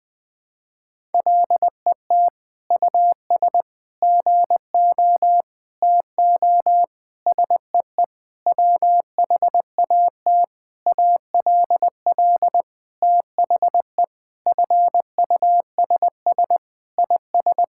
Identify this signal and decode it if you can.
Morse code, heard as a keyed tone, about 20 wpm, 700 Hz. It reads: LET US GO TO SEE WHAT ALL THE FUSS IS
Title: Sentences with 8-Word Phrases - Spoken Beginning - 20wpm